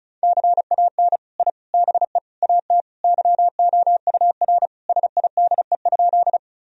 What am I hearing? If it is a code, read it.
Morse code, heard as a keyed tone, about 36 wpm, 700 Hz. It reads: CAN I BE AT YOUR SIDE?
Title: Sentences from Top 1000 Words - Review Part 1 - 35wpm